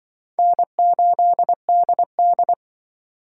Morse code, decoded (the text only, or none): N8DD